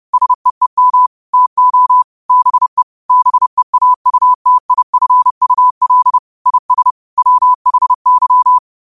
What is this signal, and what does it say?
Morse code, heard as a keyed tone, about 30 wpm, 1000 Hz. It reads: SEEM TO BE BEAUTIFUL IS WHY
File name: qotdfiles / 30wpm